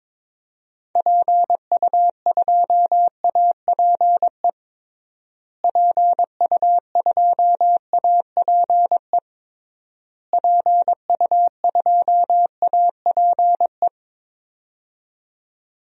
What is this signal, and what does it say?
Morse code, heard as a keyed tone, about 22 wpm, 700 Hz. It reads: PU2APE PU2APE PU2APE
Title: DXCC-100 Call Signs - 6 Characters - Repeated 3x - 22wpm